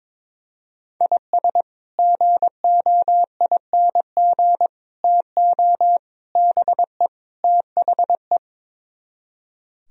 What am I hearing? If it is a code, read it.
Morse code, heard as a keyed tone, about 22 wpm, 700 Hz. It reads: IS GOING TO BE THE
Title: Speed-Racing - Sets of 5 Words 22wpm